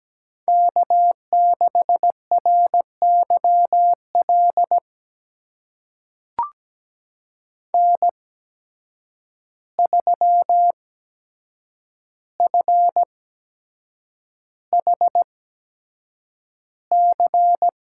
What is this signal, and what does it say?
Morse code, heard as a keyed tone, about 17 wpm, spaced as for 7 wpm, 700 Hz. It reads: K6RYL N3FHC
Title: USA Call Signs - 5 Characters - 8x Character Spacing - 17wpm